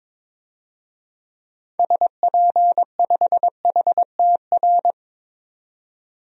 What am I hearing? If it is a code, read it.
Morse code, heard as a keyed tone, about 22 wpm, 700 Hz. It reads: SP5HTR
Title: DXCC-100 Call Signs - 6 Characters - 22wpm